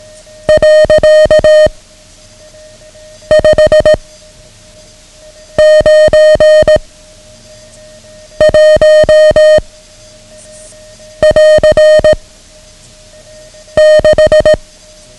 Morse code, heard as a keyed tone, about 18 wpm, 620 Hz. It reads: . 5 9 1 <AR> 6
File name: ham radio / CodeQuick / Code Quick 09